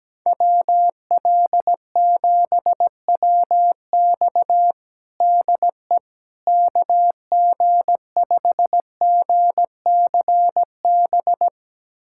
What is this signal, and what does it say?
Morse code, heard as a keyed tone, about 17 wpm, 700 Hz. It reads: WL7WX DE KG5GCB
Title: USA Callsign DE Callsign 17wpm